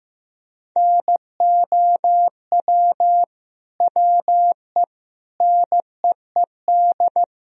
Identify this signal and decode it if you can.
Morse code, heard as a keyed tone, about 15 wpm, 700 Hz. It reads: NOW WE NEED
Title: Sentences from Top 1000 Words - Review Part 4 - 15wpm